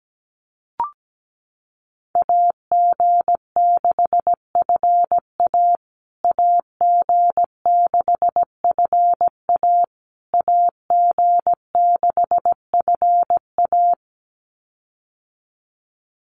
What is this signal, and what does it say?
Morse code, heard as a keyed tone, about 17 wpm, 700 Hz. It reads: AG6FA AG6FA AG6FA
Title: USA Call Signs Repeated 3x 17wpm